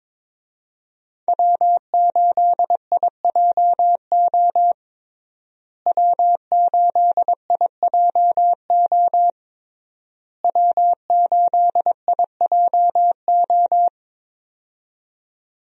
Morse code, heard as a keyed tone, about 22 wpm, 700 Hz. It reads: W8IJO W8IJO W8IJO
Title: USA Call Signs - 5 Characters - Repeated 3x - 22wpm